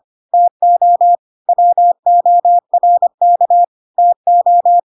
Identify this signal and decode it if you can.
Morse code, heard as a keyed tone, about 25 wpm, 700 Hz. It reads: TO WORK TO
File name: Pinocchio / Pinocchio ch3 25 25 ik1hge